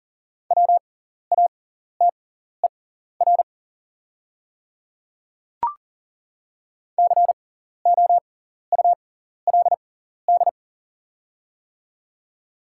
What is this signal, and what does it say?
Morse code, heard as a keyed tone, about 40 wpm, spaced as for 14 wpm, 700 Hz. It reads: WATER COULD
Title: 6x Character Spacing - Top 200 Words - 40wpm